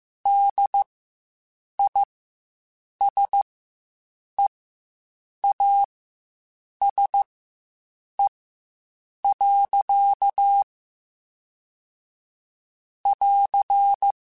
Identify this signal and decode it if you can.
Morse code, heard as a keyed tone, about 15 wpm, spaced as for 7 wpm, 800 Hz. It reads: DISEASE. <AR>
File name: Quotes Spaced